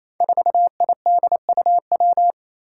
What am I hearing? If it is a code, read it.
Morse code, heard as a keyed tone, about 28 wpm, 700 Hz. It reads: 4IDUW
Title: Five Challenging Letter-Number Combinations 28wpm